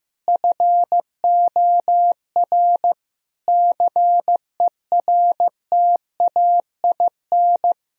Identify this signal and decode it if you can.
Morse code, heard as a keyed tone, about 15 wpm, 700 Hz. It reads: FOR CERTAIN
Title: Sentences from Top 1000 Words - Review Part 2 - 15wpm